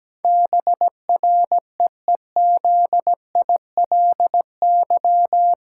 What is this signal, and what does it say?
Morse code, heard as a keyed tone, about 17 wpm, 700 Hz. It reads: BREEZILY
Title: Pangram Challenge v2 - 17wpm